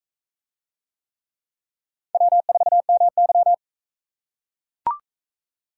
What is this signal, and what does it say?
Morse code, heard as a keyed tone, about 42 wpm, 700 Hz. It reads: W4MY